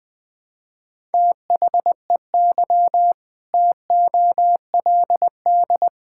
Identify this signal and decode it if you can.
Morse code, heard as a keyed tone, about 20 wpm, 700 Hz. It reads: THEY TOLD
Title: Sentences with 2-Word Phrases - 20wpm